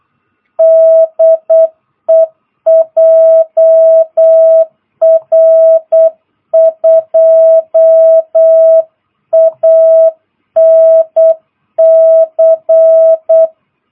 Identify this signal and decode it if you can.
Morse code, heard as a keyed tone, about 8 wpm, 660 Hz. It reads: DE JR2ANC
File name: jh8cbh / 2022cw